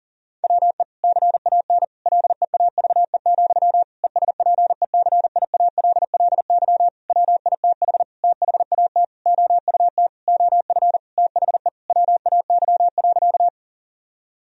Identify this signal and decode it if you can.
Morse code, heard as a keyed tone, about 40 wpm, 700 Hz. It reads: WE CAN LEAVE, ESPECIALLY WITH THAT OUT OF THE WAY.
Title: Sentences from Top 1000 Words - Review Part 4 - 40wpm